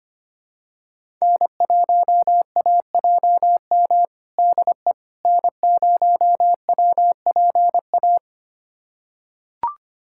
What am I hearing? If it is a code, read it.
Morse code, heard as a keyed tone, about 25 wpm, 700 Hz. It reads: N1AJM DE N0WPA
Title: USA Callsign DE Callsign 25wpm